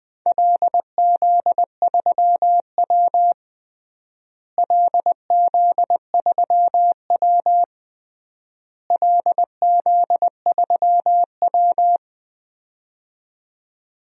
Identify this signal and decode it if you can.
Morse code, heard as a keyed tone, about 21 wpm, 700 Hz. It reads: LZ3W LZ3W LZ3W
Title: DXCC-100 Call Signs - 4 Characters - Repeated 3x - 20wpm